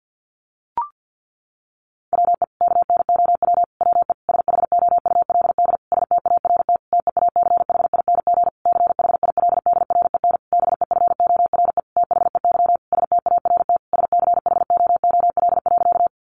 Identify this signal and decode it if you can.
Morse code, heard as a keyed tone, about 48 wpm, 700 Hz. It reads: WE KNOW WE SHOULD START TEACHING CHILDREN BEFORE THEY START SCHOOL.
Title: Sentences from Top 1100 Words 50wpm